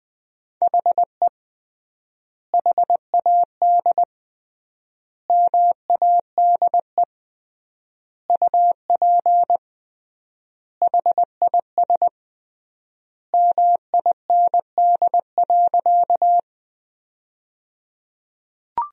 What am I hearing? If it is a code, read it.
Morse code, heard as a keyed tone, about 21 wpm, 700 Hz. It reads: HE HAD MADE UP HIS MIND.
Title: 3x Word-Spacing - Sentences from Top 500 Words - 20wpm